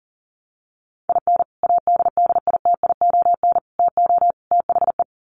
Morse code, heard as a keyed tone, about 37 wpm, 700 Hz. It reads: IN ADDITION TO THE
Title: Sets of 4 Words 40wpm